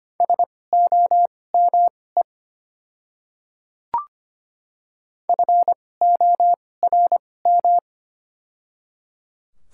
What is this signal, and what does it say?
Morse code, heard as a keyed tone, about 25 wpm, spaced as for 18 wpm, 700 Hz. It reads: SOME FORM
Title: 2x Character Spacing - Top 300 Words - 25wpm